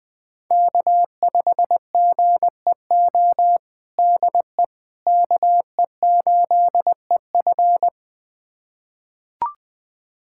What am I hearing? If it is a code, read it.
Morse code, heard as a keyed tone, about 20 wpm, 700 Hz. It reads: K5GEO DE KE8EF
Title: USA Callsign DE Callsign - 5 Characters - 20wpm